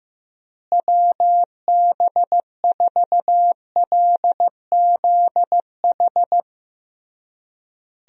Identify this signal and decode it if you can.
Morse code, heard as a keyed tone, about 15 wpm, 700 Hz. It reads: WB4LZH
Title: USA Call Signs - 6 Characters - Repeated 3x - 15wpm